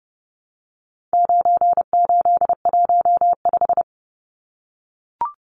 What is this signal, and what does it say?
Morse code, heard as a keyed tone, about 30 wpm, 700 Hz. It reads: 9815
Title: Numbers - 4-Digit 30wpm